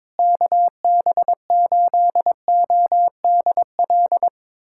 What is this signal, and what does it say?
Morse code, heard as a keyed tone, about 22 wpm, 700 Hz. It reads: KB8ODL
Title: USA Callsign DE Callsign 22wpm